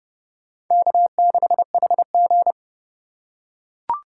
Morse code, heard as a keyed tone, about 30 wpm, 700 Hz. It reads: K6HG